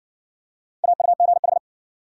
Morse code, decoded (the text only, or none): AUGH